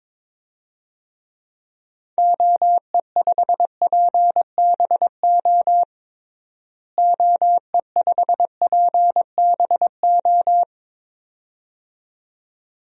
Morse code, heard as a keyed tone, about 22 wpm, 700 Hz. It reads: OE5PBO OE5PBO
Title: DXCC-100 Call Signs - 6 Characters - Repeated 2x - 22wpm